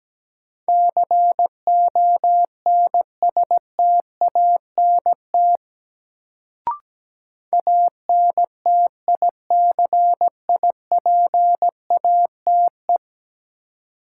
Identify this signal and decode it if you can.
Morse code, heard as a keyed tone, about 17 wpm, 700 Hz. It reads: CONSTANT ANTICIPATE
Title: Sentences from Top 1500 Words - New Word Drill - 17wpm